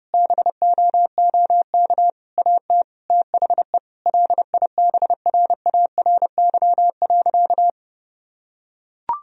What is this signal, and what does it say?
Morse code, heard as a keyed tone, about 32 wpm, 700 Hz. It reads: BOOK AT THE LIBRARY.
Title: Sentences from Top 1000 Words - Review Part 1 - 30wpm